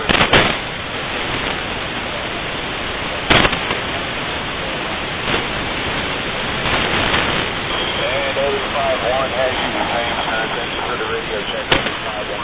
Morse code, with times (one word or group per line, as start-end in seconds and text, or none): none